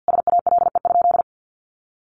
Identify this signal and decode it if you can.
Morse code, heard as a keyed tone, about 48 wpm, 700 Hz. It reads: SALE?
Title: Sentences from Top 1400 Words 50wpm